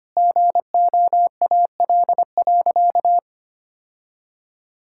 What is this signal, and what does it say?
Morse code, heard as a keyed tone, about 25 wpm, 700 Hz. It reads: GOAL.
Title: Sentences from Top 1000 Words - Review Part 5 - 25wpm